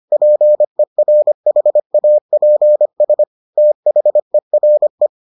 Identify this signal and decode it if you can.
Morse code, heard as a keyed tone, about 25 wpm, 600 Hz. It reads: PERHAPS THERE